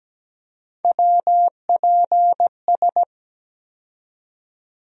Morse code, heard as a keyed tone, about 17 wpm, 700 Hz. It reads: WPS
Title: USA Call Sign Postfixes - 3 Letters 17wpm